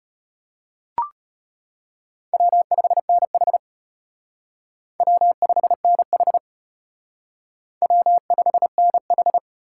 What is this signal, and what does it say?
Morse code, heard as a keyed tone, about 33 wpm, 700 Hz. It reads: W5NH W5NH W5NH